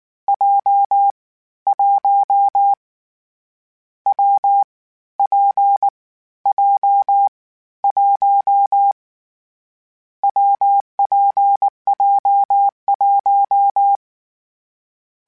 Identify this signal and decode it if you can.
Morse code, heard as a keyed tone, about 19 wpm, 800 Hz. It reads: J 1 W P J 1 WPJ1
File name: Variable Speed / Variable Speed-mp3s / WPJ WPJ1 sp @18